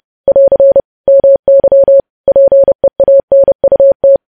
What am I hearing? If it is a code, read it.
Morse code, heard as a keyed tone, about 30 wpm, 550 Hz. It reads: <AR> MY PEANUT
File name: podcast / News headlines-1x-30x30